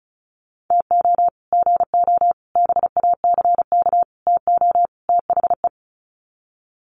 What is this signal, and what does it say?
Morse code, heard as a keyed tone, about 34 wpm, 700 Hz. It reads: TO GO BACK TO THE
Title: Sets of 5 Words 35wpm